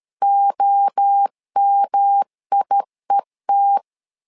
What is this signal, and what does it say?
Morse code, heard as a keyed tone, about 13 wpm, 800 Hz. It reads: OMIET
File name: LEZIONI / 4 Q 12wpm 1833